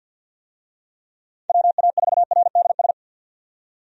Morse code, heard as a keyed tone, about 51 wpm, 700 Hz. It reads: WA4RDS
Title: USA Call Signs - 6 Characters - 50wpm